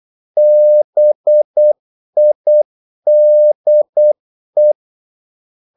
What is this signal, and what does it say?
Morse code, heard as a keyed tone, about 8 wpm, 600 Hz. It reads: BIDE
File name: Korte ord DK 8wpm